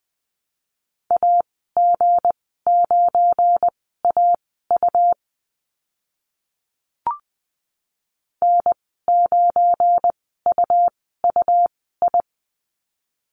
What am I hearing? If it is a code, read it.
Morse code, heard as a keyed tone, about 20 wpm, spaced as for 15 wpm, 700 Hz. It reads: AG9AU N9UUI
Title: 186 Lesson 30g - Letter b - Call Signs 20wpm